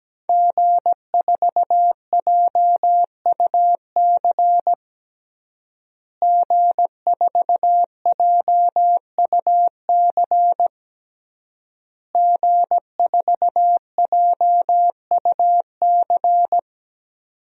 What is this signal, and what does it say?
Morse code, heard as a keyed tone, about 17 wpm, 700 Hz. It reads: G4JUC G4JUC G4JUC
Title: DXCC-100 Call Signs - 5 Characters - Repeated 3x - 17wpm